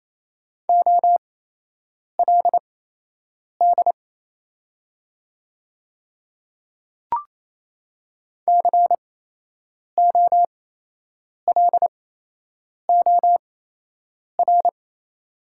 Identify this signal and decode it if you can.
Morse code, heard as a keyed tone, about 27 wpm, spaced as for 8 wpm, 690 Hz. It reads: OLD COLOR
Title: 8x Character Spacing - Top 300 Words - 28wpm